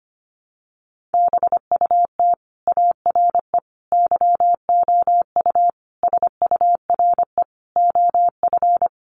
Morse code, heard as a keyed tone, about 25 wpm, 700 Hz. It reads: BUT ARE YOU SURE OF
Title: Sentences from Top 300 Words 25wpm